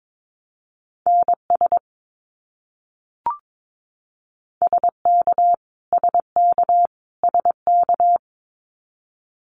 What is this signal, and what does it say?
Morse code, heard as a keyed tone, about 22 wpm, 700 Hz. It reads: NS SK SK SK
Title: Canadian Provinces and Territories - Repeated 3x - 22wpm